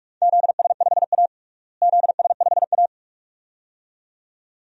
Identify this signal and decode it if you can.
Morse code, heard as a keyed tone, about 43 wpm, 700 Hz. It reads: ZS5A ZS5A
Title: DXCC-100 Call Signs - 4 Characters - Repeated 2x - 45wpm